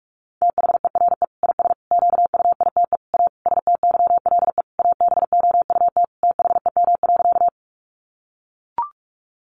Morse code, heard as a keyed tone, about 44 wpm, 700 Hz. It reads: THERE IS QUITE A STYLE ABOUT THEM.
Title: Sentences from Top 800 Words 45wpm